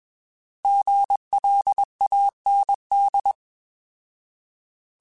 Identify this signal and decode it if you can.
Morse code, heard as a keyed tone, about 20 wpm, 800 Hz. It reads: GLAND
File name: Alphabetical / Alphabetical mp3s / Gs @20